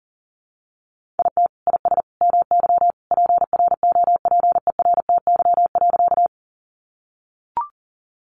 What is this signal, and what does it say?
Morse code, heard as a keyed tone, about 40 wpm, 700 Hz. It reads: IT IS MY PROPERTY.
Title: Sentences from Top 1000 Words 40wpm